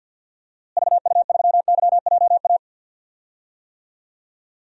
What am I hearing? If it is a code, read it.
Morse code, heard as a keyed tone, about 46 wpm, 700 Hz. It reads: VU3YJA
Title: DXCC-100 Call Signs - 6 Characters - Repeated 2x - 50wpm